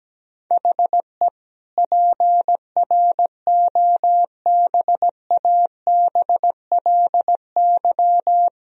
Morse code, heard as a keyed tone, about 17 wpm, 700 Hz. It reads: HE PROBABLY
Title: Sentences from Top 1000 Words - Review Part 4 - 17wpm